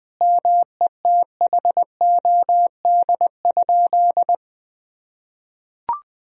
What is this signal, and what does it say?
Morse code, heard as a keyed tone, about 20 wpm, 700 Hz. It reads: METHOD?
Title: Sentences from Top 1000 Words - Review Part 3 - 20wpm